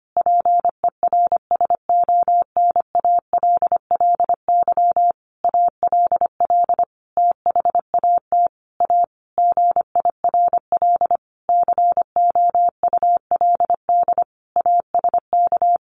Morse code, heard as a keyed tone, about 25 wpm, 700 Hz. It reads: PERSONALLY ALL THAT A GIRL COULD ASK
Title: Sentences from Top 1400 Words 25wpm